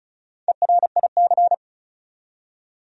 Morse code, heard as a keyed tone, about 36 wpm, 700 Hz. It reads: ERIC
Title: USA Common Names - Max 5 Letters - 35wpm